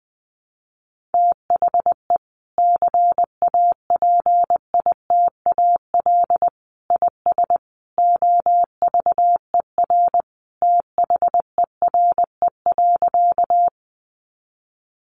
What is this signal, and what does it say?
Morse code, heard as a keyed tone, about 20 wpm, 700 Hz. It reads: THE CAPITAL IS OVER THERE.
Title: Sentences from Top 1000 Words 20wpm